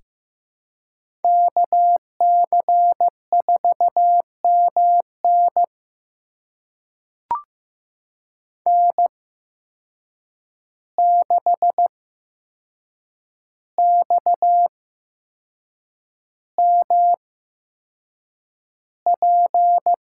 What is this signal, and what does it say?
Morse code, heard as a keyed tone, about 15 wpm, spaced as for 6 wpm, 700 Hz. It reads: KC4MN N6XMP
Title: USA Call Signs - 8x Character Spacing - 15wpm